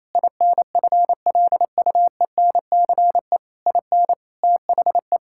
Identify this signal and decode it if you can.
Morse code, heard as a keyed tone, about 28 wpm, 700 Hz. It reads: INFLUENCE IN THE
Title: Sentences from Top 1600 Words 28wpm